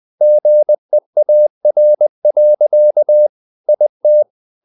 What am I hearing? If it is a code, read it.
Morse code, heard as a keyed tone, about 20 wpm, 600 Hz. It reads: GEAR. IT
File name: the old man and the sea / the old man and the sea part 1 20wpm